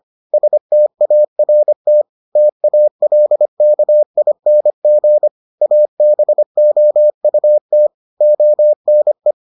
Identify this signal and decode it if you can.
Morse code, heard as a keyed tone, about 25 wpm, 600 Hz. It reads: START TALKING ABOUT ONE